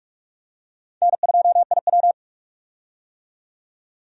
Morse code, heard as a keyed tone, about 46 wpm, 700 Hz. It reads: N2IW
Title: USA Call Signs - 4 Characters - 45wpm